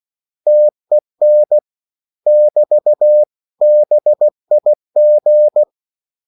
Morse code, heard as a keyed tone, about 16 wpm, 600 Hz. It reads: TEN <BT> BIG
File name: CW / Practice / 500MostCommonEnglishWords16